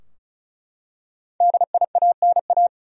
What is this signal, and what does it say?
Morse code, heard as a keyed tone, about 36 wpm, 700 Hz. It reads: DIANA
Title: USA Common Names - Max 5 Letters - 35wpm